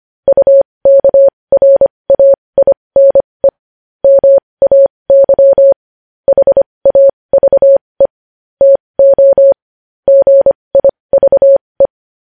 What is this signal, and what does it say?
Morse code, heard as a keyed tone, about 25 wpm, spaced as for 20 wpm, 550 Hz. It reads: UKRAINE MAY HAVE TO GIVE
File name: podcast / News headlines-1x-25x20